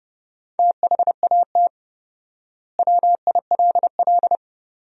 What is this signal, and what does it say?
Morse code, heard as a keyed tone, about 30 wpm, 700 Hz. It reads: THAT WILL
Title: 4x Word-Spacing - Sentences from Top 100 Words - 30wpm